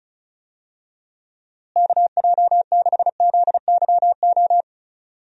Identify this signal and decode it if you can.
Morse code, heard as a keyed tone, about 36 wpm, 700 Hz. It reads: KJ6ZYO
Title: USA Call Signs - 6 Characters - Repeated 3x - 35wpm